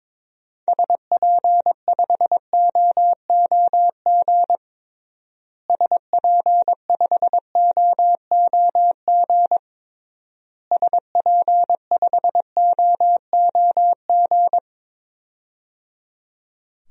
Morse code, heard as a keyed tone, about 21 wpm, 700 Hz. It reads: SP5OOG SP5OOG SP5OOG
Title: DXCC-100 Call Signs - 6 Characters - Repeated 3x - 22wpm